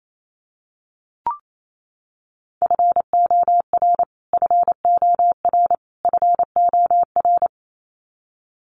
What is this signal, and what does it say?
Morse code, heard as a keyed tone, about 28 wpm, 700 Hz. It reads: FOR FOR FOR
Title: Top 500 Words Repeated 28wpm